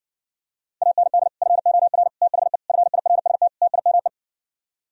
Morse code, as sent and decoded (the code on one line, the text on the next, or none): .- -. -.. / ..-. --- .-. / - .... / ..-. .. .-. ... - / - .. -- .
AND FOR THE FIRST TIME